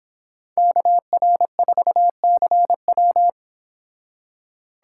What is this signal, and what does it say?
Morse code, heard as a keyed tone, about 26 wpm, 700 Hz. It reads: KR4CW